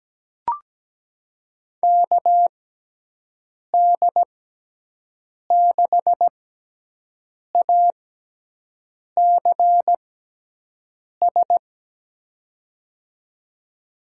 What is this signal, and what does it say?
Morse code, heard as a keyed tone, about 17 wpm, spaced as for 6 wpm, 700 Hz. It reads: KD6ACS
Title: USA Call Signs - 6 Characters - 6x Character Spacing - 17wpm